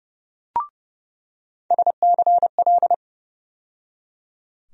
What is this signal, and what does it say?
Morse code, heard as a keyed tone, about 30 wpm, 700 Hz. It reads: SCL
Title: USA Call Sign Postfixes - 3 Letters 30wpm